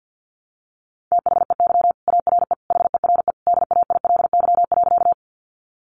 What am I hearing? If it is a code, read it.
Morse code, heard as a keyed tone, about 50 wpm, 700 Hz. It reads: THEY ARE HERE DAILY.
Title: Sentences from Top 1100 Words 50wpm